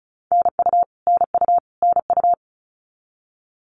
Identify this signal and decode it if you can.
Morse code, heard as a keyed tone, about 34 wpm, 700 Hz. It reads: NU NU NU
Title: Canadian Provinces and Territories - Repeated 3x - 35wpm